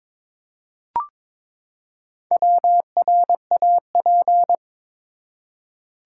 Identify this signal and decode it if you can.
Morse code, heard as a keyed tone, about 22 wpm, 700 Hz. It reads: WRAP
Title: Four Letter Words - 22wpm